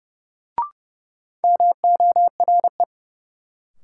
Morse code, heard as a keyed tone, about 30 wpm, 700 Hz. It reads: MORE
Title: Sets of 2 Words - New Word Drill - 30wpm